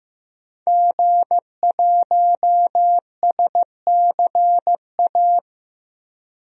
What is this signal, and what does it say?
Morse code, heard as a keyed tone, about 16 wpm, 700 Hz. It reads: G1SCA
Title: DXCC-100 Call Signs - 5 Characters - Repeated 3x - 15wpm